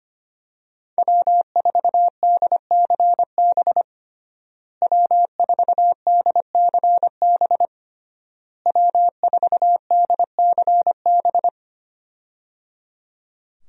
Morse code, heard as a keyed tone, about 25 wpm, 700 Hz. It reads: W4DCB W4DCB W4DCB
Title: USA Call Signs - 5 Characters - Repeated 3x - 25wpm